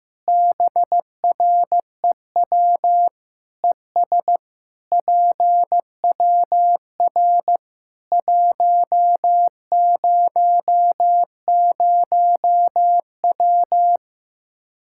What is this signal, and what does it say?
Morse code, heard as a keyed tone, about 15 wpm, 700 Hz. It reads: BREW ES PWR 100W
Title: Practice QSO - Exchange 2 - Info - 15wpm